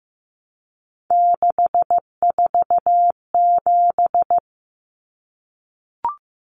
Morse code, heard as a keyed tone, about 15 wpm, 700 Hz. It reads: 647
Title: Numbers - 3-Digit 15wpm